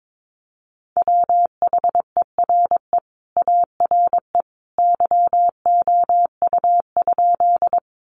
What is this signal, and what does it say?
Morse code, heard as a keyed tone, about 22 wpm, 700 Hz. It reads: WHERE ARE YOU?
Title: Sentences from Top 200 Words 22wpm